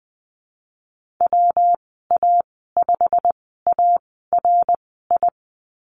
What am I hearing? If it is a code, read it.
Morse code, heard as a keyed tone, about 20 wpm, spaced as for 15 wpm, 700 Hz. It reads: WA5ARI
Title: 167 Lesson 27g - Number 7 - Call Signs 20wpm